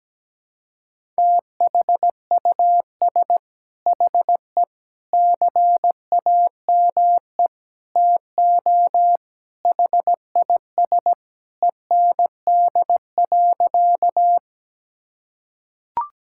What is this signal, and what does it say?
Morse code, heard as a keyed tone, about 17 wpm, 700 Hz. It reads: THUS HE CAME TO HIS END.
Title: Sentences from Top 1000 Words - Review Part 5 - 17wpm